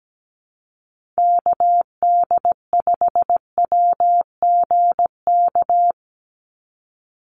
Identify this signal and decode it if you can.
Morse code, heard as a keyed tone, about 17 wpm, 700 Hz. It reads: KD5WGK
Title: USA Call Signs 17wpm